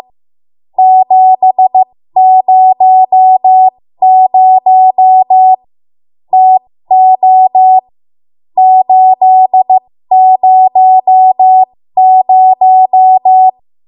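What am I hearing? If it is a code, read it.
Morse code, heard as a keyed tone, about 15 wpm, spaced as for 13 wpm, 750 Hz. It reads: 700 TO 800
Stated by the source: ARRL Morse code practice at 13 WPM.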